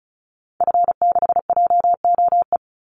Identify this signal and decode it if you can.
Morse code, heard as a keyed tone, about 34 wpm, 700 Hz. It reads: F6JOE
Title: Call Signs - 35wpm